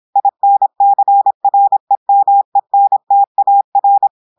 Morse code, heard as a keyed tone, about 26 wpm, 800 Hz. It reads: INCREMENTAR
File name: portugais texte en clair 2 vitesse 26 mots minutes